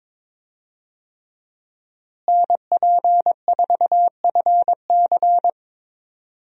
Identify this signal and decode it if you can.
Morse code, heard as a keyed tone, about 22 wpm, 700 Hz. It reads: NP4FC
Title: USA Call Signs - 5 Characters - Repeated 3x - 22wpm